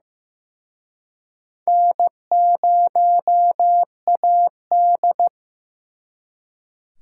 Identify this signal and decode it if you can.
Morse code, heard as a keyed tone, about 18 wpm, 700 Hz. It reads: N0AD
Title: Speed Racing - USA 1x2 Call Signs 15wpm